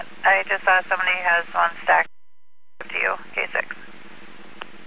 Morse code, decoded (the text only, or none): none